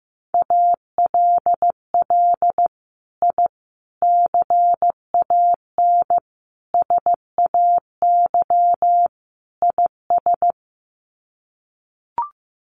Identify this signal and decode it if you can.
Morse code, heard as a keyed tone, about 15 wpm, 700 Hz. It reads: ALL I CAN SAY IS
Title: Sets of 5 Words Encore2 15wpm